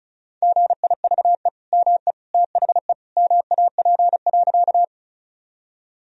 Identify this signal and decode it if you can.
Morse code, heard as a keyed tone, about 36 wpm, 700 Hz. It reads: GIVE ME THE MAP.
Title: Sentences from Top 1000 Words - Review Part 2 - 35wpm